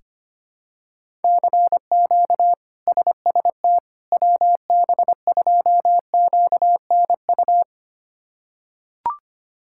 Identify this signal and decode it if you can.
Morse code, heard as a keyed tone, about 25 wpm, 700 Hz. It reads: CQ SST WB2QNU
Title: K1USN SST QSO Practice - 25wpm